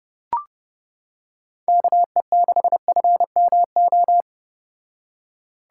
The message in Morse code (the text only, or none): KE6FMO